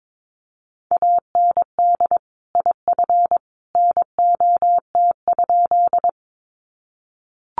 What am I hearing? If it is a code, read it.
Morse code, heard as a keyed tone, about 22 wpm, 700 Hz. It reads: AND IF NOT?
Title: Sentences from Top 600 Words 22wpm